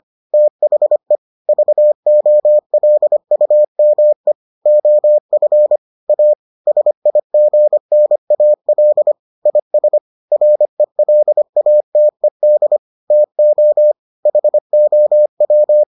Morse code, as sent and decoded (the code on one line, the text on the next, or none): - .... / ...- --- .-.. ..- -- . / --- ..-. / .- / ... .. --. -. .- .-.. / .. ... / .-. .-.. .- - . -.. / - --- / .... --- .--
THE VOLUME OF A SIGNAL IS RELATED TO HOW